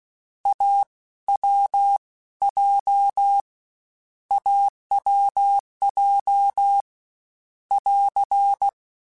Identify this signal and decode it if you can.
Morse code, heard as a keyed tone, about 16 wpm, spaced as for 13 wpm, 800 Hz. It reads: AWJ AWJ <AR>
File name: Difficult Combos / Difficult Combos-mp3s / AWJ1 @15